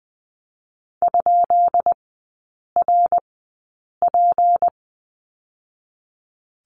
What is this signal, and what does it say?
Morse code, heard as a keyed tone, about 20 wpm, spaced as for 8 wpm, 700 Hz. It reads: ?RP
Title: 152 Lesson 25e - Letter p - Three Characters 20wpm